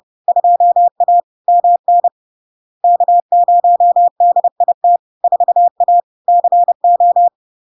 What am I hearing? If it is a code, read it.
Morse code, heard as a keyed tone, about 30 wpm, 700 Hz. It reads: 2A MN K0DIT 4A CO
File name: LARK-FD-2022-Log-in-CW-30wpm-700-Hz